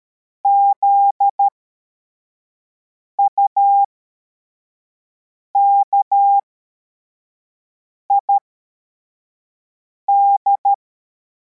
Morse code, heard as a keyed tone, about 13 wpm, spaced as for 4 wpm, 800 Hz. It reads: ZUKID
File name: formation / cw / Text44